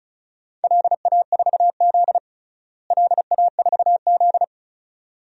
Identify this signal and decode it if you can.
Morse code, heard as a keyed tone, about 36 wpm, 700 Hz. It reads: LA4Z LA4Z
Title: DXCC-100 Call Signs - 4 Characters - Repeated 2x - 35wpm